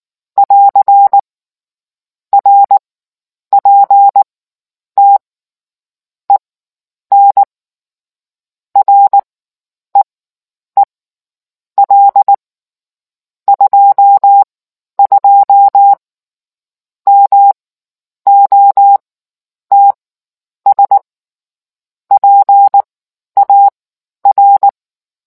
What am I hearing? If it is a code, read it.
Morse code, heard as a keyed tone, about 19 wpm, spaced as for 9 wpm, 800 Hz. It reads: <AR> RPT EN REEL 22 MOTS PAR